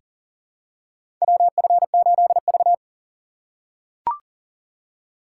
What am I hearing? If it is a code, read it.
Morse code, heard as a keyed tone, about 40 wpm, 700 Hz. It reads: WF8V
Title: USA Call Signs - 4 Characters - 6x Character Spacing - 40wpm